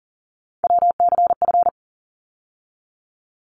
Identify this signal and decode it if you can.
Morse code, heard as a keyed tone, about 40 wpm, 700 Hz. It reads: WCF
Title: ARRL Sections 40wpm